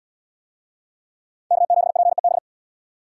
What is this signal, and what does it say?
Morse code, heard as a keyed tone, about 72 wpm, 700 Hz. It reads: K7PEB